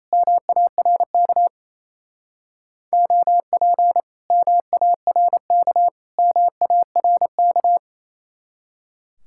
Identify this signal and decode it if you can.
Morse code, heard as a keyed tone, about 35 wpm, 700 Hz. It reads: MARK OP MARK MARK
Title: Speed-Racing Common QSO Phrases Encore - 28wpm